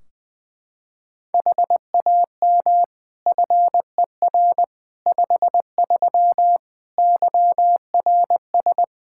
Morse code, heard as a keyed tone, about 21 wpm, 700 Hz. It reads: HAM FER 53 YRS
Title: Practice QSO - Exchange 3 - More Info - 20wpm